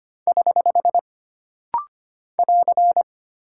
Morse code, heard as a keyed tone, about 25 wpm, 700 Hz. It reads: <HH> <AR>
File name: Single Punctuation-Prosign - Rapid-Fire - 25wpm